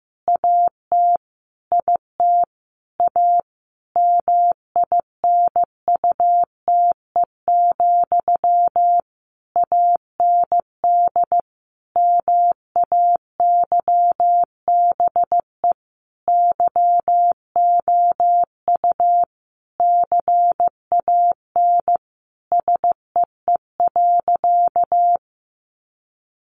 Morse code, heard as a keyed tone, about 15 wpm, 700 Hz. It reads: AT IT A MINUTE, AND MAYBE YOU CAN SEE.
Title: Sentences from Top 1000 Words 15wpm